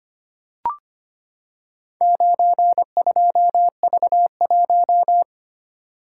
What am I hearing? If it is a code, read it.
Morse code, heard as a keyed tone, about 25 wpm, 700 Hz. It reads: 92V1
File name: practice / 25wpm / Four Letter-Number Combinations - 25wpm